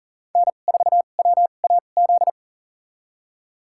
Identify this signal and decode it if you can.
Morse code, heard as a keyed tone, about 40 wpm, spaced as for 29 wpm, 700 Hz. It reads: N4WAZ
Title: USA Call Signs - 5 Characters - 2x Character Spacing - 40wpm